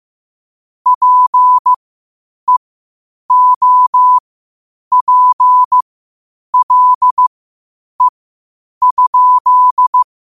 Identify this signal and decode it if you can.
Morse code, heard as a keyed tone, about 15 wpm, spaced as for 8 wpm, 1000 Hz. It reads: PEOPLE?
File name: qotdfiles / 8wpm